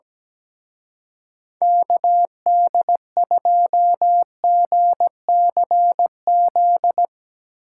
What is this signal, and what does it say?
Morse code, heard as a keyed tone, about 19 wpm, 700 Hz. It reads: KD2GCZ
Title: Speed Racing - USA 2x3 Call Signs 17wpm